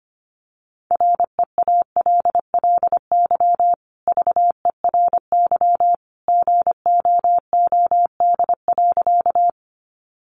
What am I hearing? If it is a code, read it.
Morse code, heard as a keyed tone, about 25 wpm, 700 Hz. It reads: REALLY VERY GOOD.
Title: Sentences from Top 600 Words 25wpm